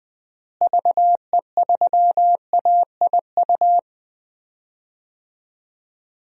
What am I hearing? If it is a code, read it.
Morse code, heard as a keyed tone, about 20 wpm, 700 Hz. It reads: VE3AIU
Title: DXCC-100 Call Signs - 6 Characters - Repeated 2x - 20wpm